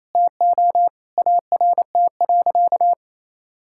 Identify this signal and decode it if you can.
Morse code, heard as a keyed tone, about 27 wpm, 700 Hz. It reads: TO ART.
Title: Sentences from Top 1000 Words - Review Part 6 - 28wpm